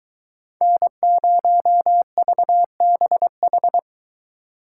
Morse code, heard as a keyed tone, about 22 wpm, 700 Hz. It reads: N0VBH